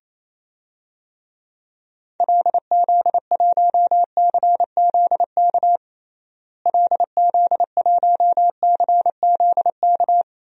Morse code, heard as a keyed tone, about 28 wpm, 700 Hz. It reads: LZ1CZK LZ1CZK
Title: DXCC-100 Call Signs - 6 Characters - Repeated 3x - 28wpm